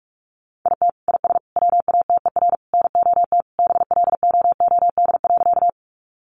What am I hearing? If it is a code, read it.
Morse code, heard as a keyed tone, about 44 wpm, 700 Hz. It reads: IT IS WATER NOT BLOOD.
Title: Sentences from Top 1100 Words 45wpm